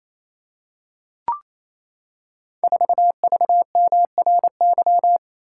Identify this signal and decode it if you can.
Morse code, heard as a keyed tone, about 27 wpm, 700 Hz. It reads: 4VMRY
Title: Code Groups - 5 Letter:Numbers - 28wpm